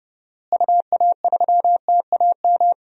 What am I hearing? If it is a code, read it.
Morse code, heard as a keyed tone, about 30 wpm, 700 Hz. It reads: UA3TAM
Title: DXCC-100 Call Signs - 6 Characters - Repeated 2x - 30wpm